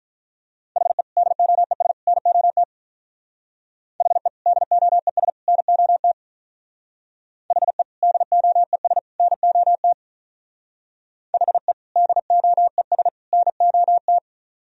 Morse code, heard as a keyed tone, about 43 wpm, 700 Hz. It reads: HE DOES NOT HE DOES NOT HE DOES NOT HE DOES NOT
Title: Speed-Racing - Sets of 3 Words 35wpm